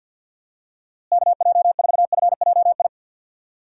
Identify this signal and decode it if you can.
Morse code, heard as a keyed tone, about 46 wpm, 700 Hz. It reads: KJ4FJI
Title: USA Call Signs - 6 Characters - Repeated 2x - 50wpm